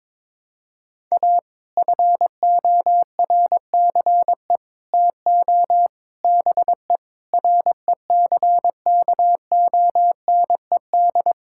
Morse code, heard as a keyed tone, about 22 wpm, 700 Hz. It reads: A FORCE TO BE RECKONED
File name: practice / 22wpm / Sets of 6 Words - 22wpm